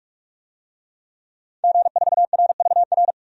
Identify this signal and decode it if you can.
Morse code, heard as a keyed tone, about 46 wpm, 700 Hz. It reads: G4RVR
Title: DXCC-100 Call Signs - 5 Characters - 45wpm